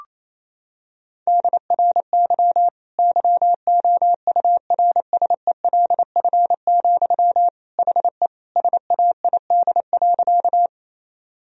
Morse code, heard as a keyed tone, about 28 wpm, 700 Hz. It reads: DRY YOURSELF, HE SAID.
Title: Sentences from Top 1000 Words - Review Part 3 - 28wpm